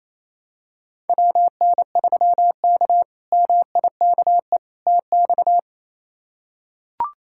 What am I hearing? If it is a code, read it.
Morse code, heard as a keyed tone, about 28 wpm, 700 Hz. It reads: WN3K MIKE TX
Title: K1USN SST QSO Practice - 28wpm